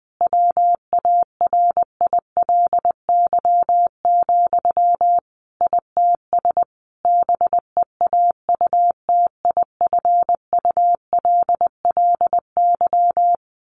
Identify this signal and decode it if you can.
Morse code, heard as a keyed tone, about 20 wpm, 700 Hz. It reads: WARILY, ITS BEAUTIFULLY
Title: A Sound Of Thunder 20wpm